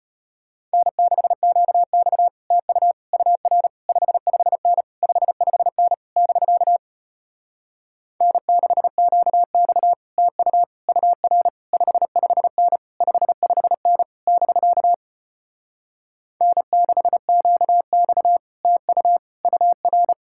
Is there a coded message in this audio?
Yes: Morse code from 0.7 to 20.1 s, heard as a keyed tone, about 33 wpm, 700 Hz. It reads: N6QX TU UR 55N 55N <BK> N6QX TU UR 55N 55N <BK> N6QX TU UR